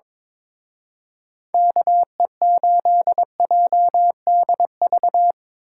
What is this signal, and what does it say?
Morse code, heard as a keyed tone, about 24 wpm, 700 Hz. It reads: KE8JDV
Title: Speed Racing - USA 2x3 Call Signs 22wpm